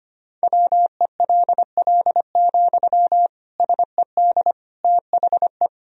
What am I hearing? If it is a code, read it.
Morse code, heard as a keyed tone, about 25 wpm, 700 Hz. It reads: WELL, SED THE
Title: Sentences from Top 400 Words and Abbreviations 25wpm